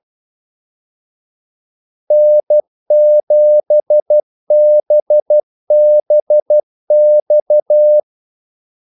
Morse code, heard as a keyed tone, about 12 wpm, 600 Hz. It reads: N7BBX
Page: 2 ekstra ordmellemrum mellem kaldesignalerne